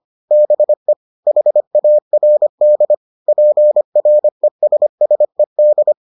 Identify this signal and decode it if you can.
Morse code, heard as a keyed tone, about 25 wpm, 600 Hz. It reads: BE HARD PRESSED